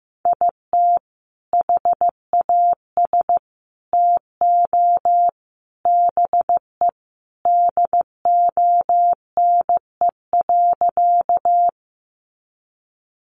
Morse code, heard as a keyed tone, about 14 wpm, 700 Hz. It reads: IT HAS TO BE DONE.
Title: Sentences from Top 500 Words 15wpm